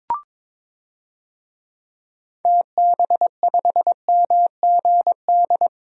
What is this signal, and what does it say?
Morse code, heard as a keyed tone, about 22 wpm, 700 Hz. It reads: TB5MGD
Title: DXCC-100 Call Signs - 6 Characters - Repeated 2x - 22wpm